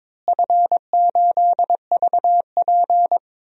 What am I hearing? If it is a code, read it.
Morse code, heard as a keyed tone, about 22 wpm, 700 Hz. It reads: F8VP
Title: DXCC-100 Call Signs - 4 Characters - Repeated 3x - 22wpm